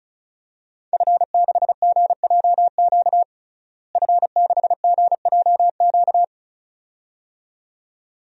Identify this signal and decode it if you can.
Morse code, heard as a keyed tone, about 36 wpm, 700 Hz. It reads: F6GJQ F6GJQ
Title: DXCC-100 Call Signs - 5 Characters - Repeated 2x - 35wpm